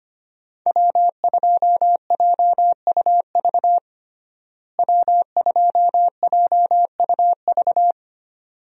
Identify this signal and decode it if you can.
Morse code, heard as a keyed tone, about 25 wpm, 700 Hz. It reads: W2JUV W2JUV
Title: USA Call Signs - 5 Characters - Repeated 2x - 25wpm